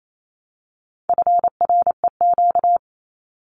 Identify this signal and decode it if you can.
Morse code, heard as a keyed tone, about 28 wpm, 700 Hz. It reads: FREQ
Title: Abbreviations Repeated 3x - 28wpm